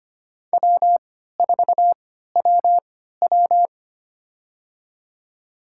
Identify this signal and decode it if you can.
Morse code, heard as a keyed tone, about 25 wpm, spaced as for 14 wpm, 700 Hz. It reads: W4WW
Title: USA Call Signs - 3x Character Spacing - 25wpm